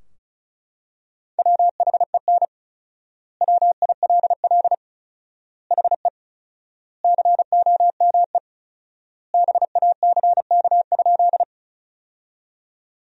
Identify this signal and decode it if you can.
Morse code, heard as a keyed tone, about 34 wpm, 700 Hz. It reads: WHEN WILL HE COME BACK?
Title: 4x Word-Spacing - Sentences from Top 200 Words - 35wpm